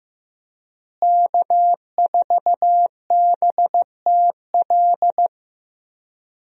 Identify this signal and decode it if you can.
Morse code, heard as a keyed tone, about 15 wpm, 700 Hz. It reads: K4BTL
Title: USA Call Signs - 5 Characters - Repeated 3x - 15wpm